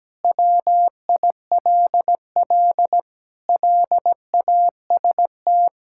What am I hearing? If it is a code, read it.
Morse code, heard as a keyed tone, about 17 wpm, 700 Hz. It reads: WILL LAST
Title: Sentences from Top 1600 Words 17wpm